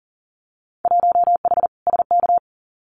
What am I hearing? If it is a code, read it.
Morse code, heard as a keyed tone, about 40 wpm, 700 Hz. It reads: 1H SK
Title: Winter Field Day Exchanges 40wpm